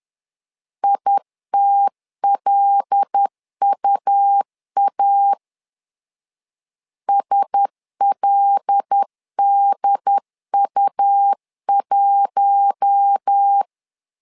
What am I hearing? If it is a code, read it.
Morse code, heard as a keyed tone, about 10 wpm, 800 Hz. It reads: ITLUA SLDU1
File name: LEZIONI / 5 LBVC / 5 L 10wpm 2259